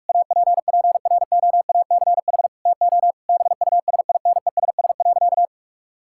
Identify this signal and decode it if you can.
Morse code, heard as a keyed tone, about 46 wpm, 700 Hz. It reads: APPROACH TO BUSINESS.
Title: Sentences from Top 1000 Words - Review Part 6 - 45wpm